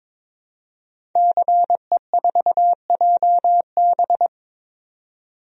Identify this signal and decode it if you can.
Morse code, heard as a keyed tone, about 22 wpm, 700 Hz. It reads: CE4JB
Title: DXCC-100 Call Signs - 5 Characters - 22wpm